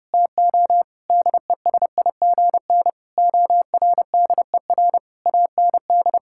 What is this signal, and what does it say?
Morse code, heard as a keyed tone, about 31 wpm, 700 Hz. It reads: TO DESIGN ORDER AND
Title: Sentences from Top 1000 Words - Review Part 6 - 30wpm